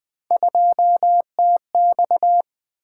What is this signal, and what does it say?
Morse code, heard as a keyed tone, about 24 wpm, 700 Hz. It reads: 2TX
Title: Speed-Racing - Three Letter-Number Combinations - 20wpm